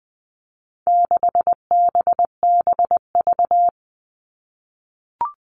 Morse code, heard as a keyed tone, about 20 wpm, 700 Hz. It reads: 6BBV
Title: All Cut Numbers - 4-Digit 20wpm